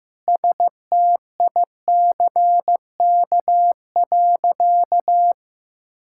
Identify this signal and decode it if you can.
Morse code, heard as a keyed tone, about 15 wpm, 700 Hz. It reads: STICK.
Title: Sentences from Top 1600 Words 15wpm